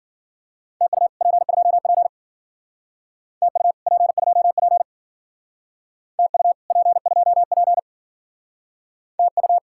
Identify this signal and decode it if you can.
Morse code, heard as a keyed tone, about 52 wpm, 700 Hz. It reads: TU P2P TU P2P TU P2P TU